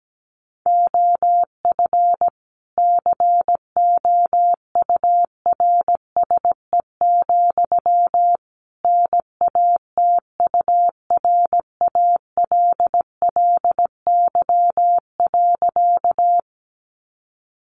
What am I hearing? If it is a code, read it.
Morse code, heard as a keyed tone, about 17 wpm, 700 Hz. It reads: OF COURSE, NATURALLY.
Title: Sentences from Top 1300 Words 17wpm